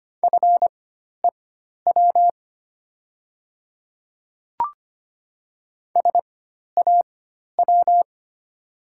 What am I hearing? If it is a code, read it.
Morse code, heard as a keyed tone, about 25 wpm, spaced as for 12 wpm, 700 Hz. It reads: FEW SAW
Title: 4x Character Spacing - Top 300 Words - 25wpm